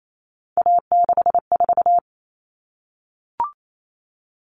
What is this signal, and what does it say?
Morse code, heard as a keyed tone, about 28 wpm, 700 Hz. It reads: A64
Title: Common Cut Numbers - 3-Digit 28wpm